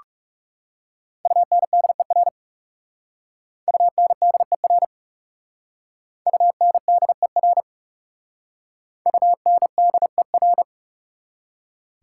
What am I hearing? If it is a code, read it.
Morse code, heard as a keyed tone, about 37 wpm, 700 Hz. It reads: UNDER UNDER UNDER UNDER
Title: Speed-Racing Prefixes and Suffixes 30wpm